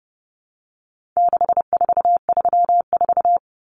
Morse code, heard as a keyed tone, about 30 wpm, 700 Hz. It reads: 6434
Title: Numbers - 4-Digit 30wpm